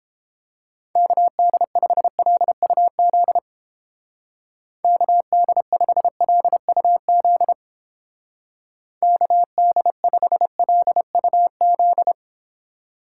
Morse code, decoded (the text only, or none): KD5LUZ KD5LUZ KD5LUZ